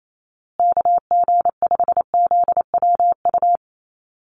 Morse code, heard as a keyed tone, about 28 wpm, 700 Hz. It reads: KG5ZWU
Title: USA Call Signs Repeated 3x 28wpm